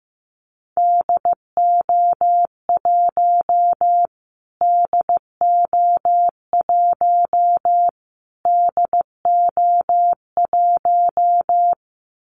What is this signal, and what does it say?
Morse code, heard as a keyed tone, about 15 wpm, 700 Hz. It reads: DO1 DO1 DO1
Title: Callsign Prefixes for Top 25 DXCC - Repeated 3x - 15wpm